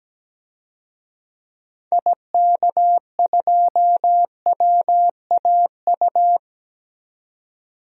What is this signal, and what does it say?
Morse code, heard as a keyed tone, about 17 wpm, 700 Hz. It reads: IK2WAU
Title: DXCC-100 Call Signs - 6 Characters - 17wpm